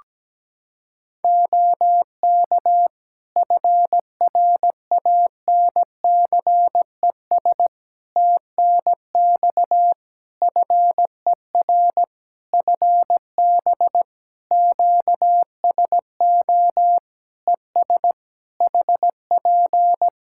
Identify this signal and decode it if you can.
Morse code, heard as a keyed tone, about 17 wpm, 700 Hz. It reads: OK FRANCES TNX FER FB QSO ES HP
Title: Practice QSO - Exchange 4 - Ending - 17wpm